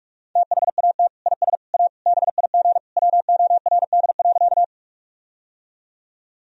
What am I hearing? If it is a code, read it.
Morse code, heard as a keyed tone, about 47 wpm, 700 Hz. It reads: THAT IS A BIG WORD.
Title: Sentences from Top 1000 Words - Review Part 1 - 45wpm